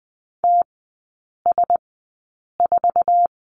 Morse code, heard as a keyed tone, about 20 wpm, spaced as for 8 wpm, 700 Hz. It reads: TS4
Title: 093 Lesson 16e - Letter c - Three Characters 20wpm